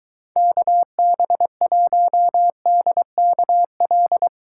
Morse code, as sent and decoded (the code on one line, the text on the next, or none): -.- -... .---- -.. -.- .-..
KB1DKL